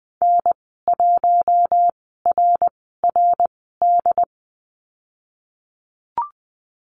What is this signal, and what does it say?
Morse code, heard as a keyed tone, about 19 wpm, spaced as for 14 wpm, 700 Hz. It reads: N1RRD
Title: 076 Lesson 13g - Letter L - Call Signs 20wpm